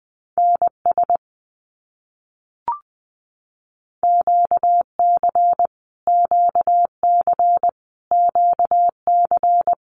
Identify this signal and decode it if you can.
Morse code, heard as a keyed tone, about 19 wpm, 700 Hz. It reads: NS QC QC QC
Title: Canadian Provinces and Territories - Repeated 3x - 20wpm